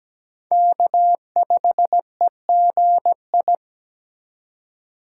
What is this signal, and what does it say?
Morse code, heard as a keyed tone, about 17 wpm, 700 Hz. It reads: K5EGI
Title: USA Call Signs - 5 Characters - 3x Character Spacing - 17wpm